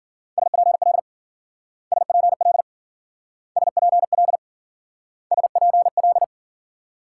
Morse code, heard as a keyed tone, about 48 wpm, 700 Hz. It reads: SPL SPL SPL SPL
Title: Speed-Racing Multi-letter Phonemes - 40wpm